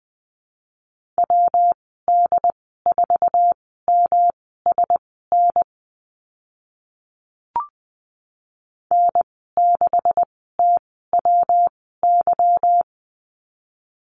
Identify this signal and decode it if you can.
Morse code, heard as a keyed tone, about 19 wpm, spaced as for 14 wpm, 700 Hz. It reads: WD4MSN N6TWY
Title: 147 Lesson 24g - Letter y - Call Signs 20wpm